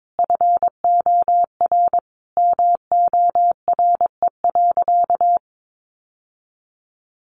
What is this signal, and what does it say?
Morse code, heard as a keyed tone, about 22 wpm, 700 Hz. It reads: FOR MORE.
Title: Sentences from Top 1100 Words 22wpm